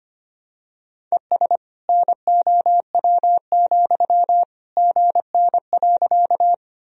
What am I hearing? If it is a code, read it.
Morse code, heard as a keyed tone, about 25 wpm, 700 Hz. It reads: ES NOW, GN.
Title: Sentences from Top 300 Words and Abbreviations 25wpm